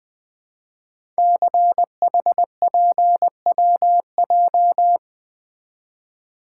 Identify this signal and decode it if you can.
Morse code, heard as a keyed tone, about 20 wpm, 700 Hz. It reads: CHPWJ
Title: Code Groups - 5 Letters - 20wpm